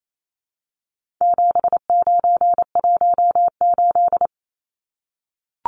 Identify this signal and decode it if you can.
Morse code, heard as a keyed tone, about 28 wpm, 700 Hz. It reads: 7918
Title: Numbers - 4-Digit 28wpm